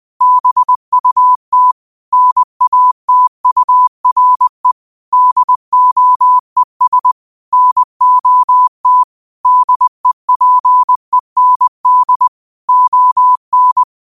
Morse code, heard as a keyed tone, about 20 wpm, 1000 Hz. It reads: BUT NATURE DOES NOT DEPEND ON
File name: qotdfiles / 20wpm